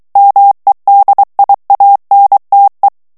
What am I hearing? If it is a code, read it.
Morse code, heard as a keyed tone, about 23 wpm, 800 Hz. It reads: MEDIANTE
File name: morse mp3 / ESPAGNOL TEXTE EN CLAIR 2 VITESSE 22 MOTS MINUTES